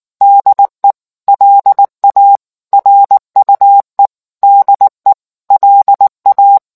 Morse code, heard as a keyed tone, about 19 wpm, 800 Hz. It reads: DE LA RUE DE LA